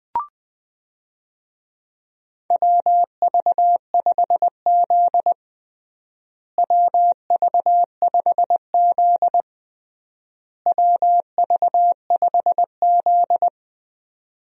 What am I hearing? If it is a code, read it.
Morse code, heard as a keyed tone, about 20 wpm, 700 Hz. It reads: WV5Z WV5Z WV5Z
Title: USA Call Signs - 4 Characters - Repeated 3x - 20wpm